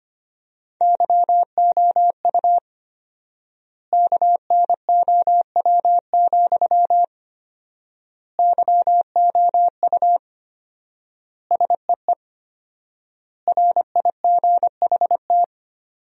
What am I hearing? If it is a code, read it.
Morse code, heard as a keyed tone, about 25 wpm, 700 Hz. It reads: YOU KNOW, YOU SEE RIGHT
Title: 4x Word-Spacing - Sentences from Top 200 Words - 25wpm